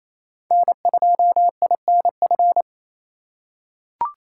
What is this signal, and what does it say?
Morse code, heard as a keyed tone, about 28 wpm, 700 Hz. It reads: N2INF
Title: USA Call Signs - 5 Characters - 4x Character Spacing - 28wpm